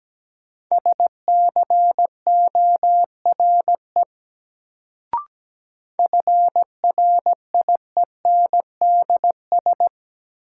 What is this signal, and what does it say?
Morse code, heard as a keyed tone, about 17 wpm, 700 Hz. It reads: SCORE FRIENDS
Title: Sentences from Top 1600 Words - New Word Drill Alt - 17wpm